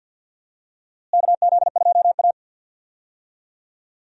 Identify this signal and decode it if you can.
Morse code, heard as a keyed tone, about 52 wpm, 700 Hz. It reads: KZ2A
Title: USA Call Signs - 4 Characters - Repeated 3x - 50wpm